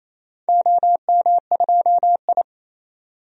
Morse code, heard as a keyed tone, about 28 wpm, 700 Hz. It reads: OM2I
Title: DXCC-100 Call Signs - 4 Characters - 28wpm